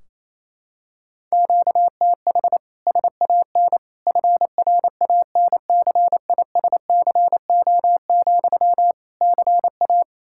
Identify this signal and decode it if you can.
Morse code, heard as a keyed tone, about 28 wpm, 700 Hz. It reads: QTH SAN FRANCISCO, CA
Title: Practice QSO - Exchange 1 - Beginning - 28wpm